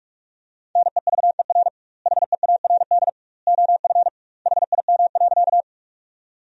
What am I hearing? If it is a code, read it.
Morse code, heard as a keyed tone, about 46 wpm, 700 Hz. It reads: NEVER HEARD OF HIM.
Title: 2x Word-Spacing - Sentences from Top 400 Words - 45wpm